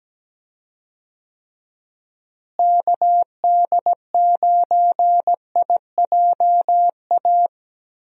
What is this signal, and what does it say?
Morse code, heard as a keyed tone, about 17 wpm, 700 Hz. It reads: KD9IJA
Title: USA Call Signs - 6 Characters - Repeated 3x - 17wpm